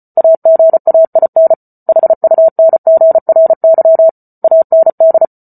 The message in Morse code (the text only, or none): AGAIN HUNGRY AND